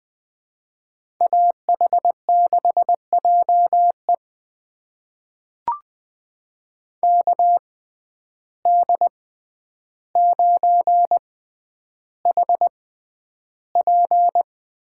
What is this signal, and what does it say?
Morse code, heard as a keyed tone, about 20 wpm, spaced as for 10 wpm, 700 Hz. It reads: AH6JE KD9HP
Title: USA Call Signs - 5 Characters - 6x Character Spacing - 20wpm